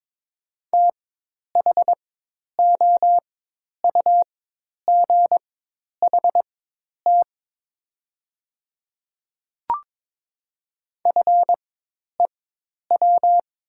Morse code, heard as a keyed tone, about 22 wpm, spaced as for 10 wpm, 700 Hz. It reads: THOUGHT FEW
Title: 4x Character Spacing - Top 500 Words - 22wpm